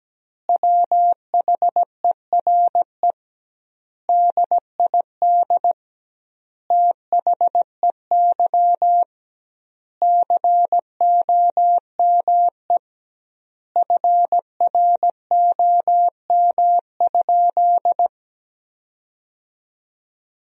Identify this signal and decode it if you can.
Morse code, heard as a keyed tone, about 17 wpm, 700 Hz. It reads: WHERE DID THEY COME FROM?
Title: 2x Word-Spacing - Sentences from Top 200 Words - 17wpm